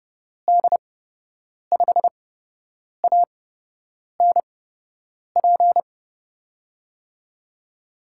Morse code, heard as a keyed tone, about 30 wpm, spaced as for 8 wpm, 700 Hz. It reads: D5ANP
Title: USA Call Signs - 6 Characters - 8x Character Spacing - 30wpm